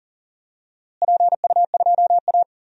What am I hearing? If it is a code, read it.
Morse code, heard as a keyed tone, about 40 wpm, 700 Hz. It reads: PU2A
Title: DXCC-100 Call Signs - 4 Characters - Repeated 2x - 40wpm